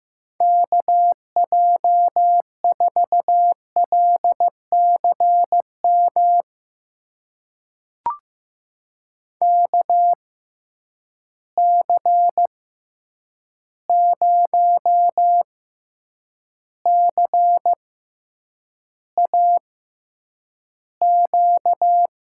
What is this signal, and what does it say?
Morse code, heard as a keyed tone, about 15 wpm, spaced as for 8 wpm, 700 Hz. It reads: KJ4LCM KC0CAQ
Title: USA Call Signs - 6 Characters - 6x Character Spacing - 15wpm